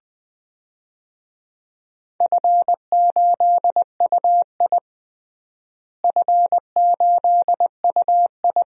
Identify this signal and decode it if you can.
Morse code, heard as a keyed tone, about 21 wpm, 700 Hz. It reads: F8UI F8UI
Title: DXCC-100 Call Signs - 4 Characters - Repeated 2x - 20wpm